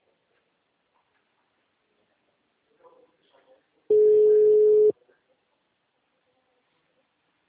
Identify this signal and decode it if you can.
Ringback tone (Europe, 425 Hz), with speech.